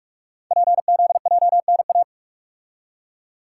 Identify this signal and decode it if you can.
Morse code, heard as a keyed tone, about 44 wpm, 690 Hz. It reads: PZJNA
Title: Code Groups - 5 Letters - 45wpm